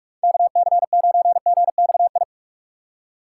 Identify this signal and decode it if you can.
Morse code, heard as a keyed tone, about 44 wpm, 700 Hz. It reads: KC9GXI